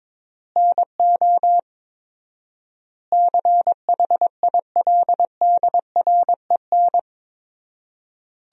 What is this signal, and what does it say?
Morse code, heard as a keyed tone, about 22 wpm, 700 Hz. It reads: NO CHILDREN
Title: 4x Word-Spacing - Sentences from Top 400 Words - 22wpm